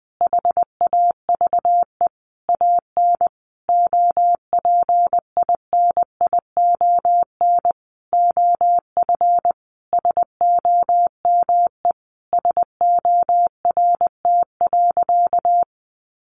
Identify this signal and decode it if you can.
Morse code, heard as a keyed tone, about 20 wpm, 700 Hz. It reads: HAVE AN OPINION OF SOME SORT.
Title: Sentences from Top 1300 Words 20wpm